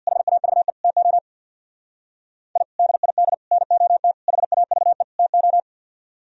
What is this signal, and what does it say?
Morse code, heard as a keyed tone, about 62 wpm, 700 Hz. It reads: HAVE TO I DID NOT HAVE TO
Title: Speed-Racing - Sets of 5 Words 50wpm